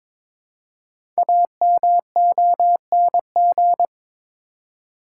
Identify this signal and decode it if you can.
Morse code, heard as a keyed tone, about 22 wpm, 700 Hz. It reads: AMONG
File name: 5 Letter Words - First 3 Letters Given - 22wpm